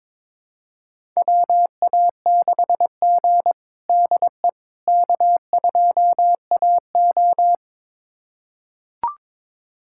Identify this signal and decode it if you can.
Morse code, heard as a keyed tone, about 22 wpm, 700 Hz. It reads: WA6G DE K2AO
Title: USA Callsign DE Callsign - 4 Characters - 22wpm